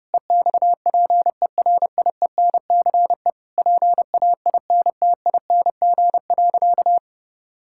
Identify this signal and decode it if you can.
Morse code, heard as a keyed tone, about 30 wpm, 700 Hz. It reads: EXPERIENCE PAINTING.
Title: Sentences from Top 1000 Words - Review Part 6 - 30wpm